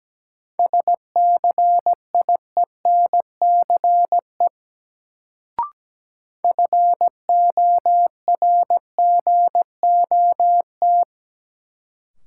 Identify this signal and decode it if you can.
Morse code, heard as a keyed tone, about 17 wpm, 700 Hz. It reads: SCIENCE FORGOT
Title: Sentences from Top 1400 Words - New Word Drill Alt - 17wpm